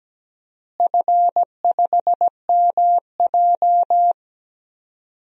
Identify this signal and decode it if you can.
Morse code, heard as a keyed tone, about 17 wpm, 700 Hz. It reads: F5MJ
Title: DXCC-100 Call Signs - 4 Characters - Repeated 2x - 17wpm